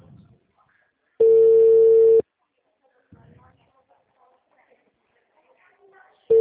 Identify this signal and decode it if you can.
Ringback tone (China, 450 Hz).